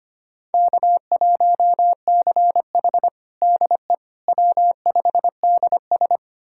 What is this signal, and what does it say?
Morse code, heard as a keyed tone, about 25 wpm, 700 Hz. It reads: K1CH DE W5DS
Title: USA Callsign DE Callsign - 4 Characters - 25wpm